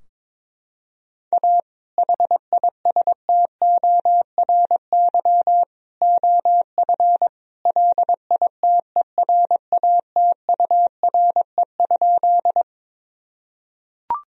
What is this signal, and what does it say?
Morse code, heard as a keyed tone, about 22 wpm, 700 Hz. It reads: A HISTORY OF LITERATURE?
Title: Sentences from Top 1000 Words - Review Part 3 - 22wpm